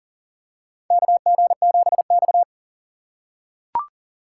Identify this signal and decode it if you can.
Morse code, heard as a keyed tone, about 40 wpm, 700 Hz. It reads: KG7X
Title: USA Call Signs - 4 Characters - 6x Character Spacing - 40wpm